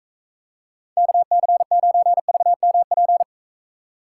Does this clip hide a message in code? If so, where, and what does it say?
Yes: Morse code from 1.0 to 3.2 s, heard as a keyed tone, about 43 wpm, 700 Hz. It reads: KC9VMP